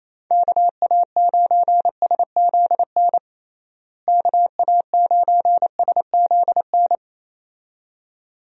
Morse code, heard as a keyed tone, about 28 wpm, 700 Hz. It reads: KA9SZN KA9SZN
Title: USA Call Signs - 6 Characters - Repeated 2x - 28wpm